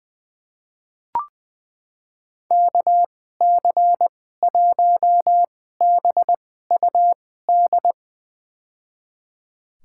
Morse code, heard as a keyed tone, about 20 wpm, spaced as for 15 wpm, 700 Hz. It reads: KC1BUD
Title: USA Call Signs - 6 Characters - 2x Character Spacing - 20wpm